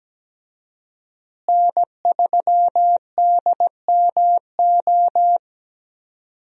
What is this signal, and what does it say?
Morse code, heard as a keyed tone, about 17 wpm, 700 Hz. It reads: N3DMO
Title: USA Call Signs - 5 Characters - Repeated 3x - 17wpm